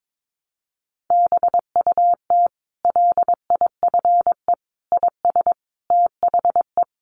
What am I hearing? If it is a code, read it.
Morse code, heard as a keyed tone, about 22 wpm, 700 Hz. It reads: BUT LIFE IS THE
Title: Sentences from Top 900 Words 22wpm